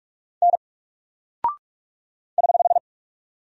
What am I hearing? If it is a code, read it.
Morse code, heard as a keyed tone, about 44 wpm, 700 Hz. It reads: N <HH>
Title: Single Character (Letter:Number:Punctuation:Prosign) 45wpm